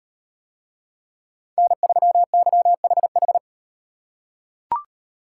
Morse code, heard as a keyed tone, about 36 wpm, 700 Hz. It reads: N3YHH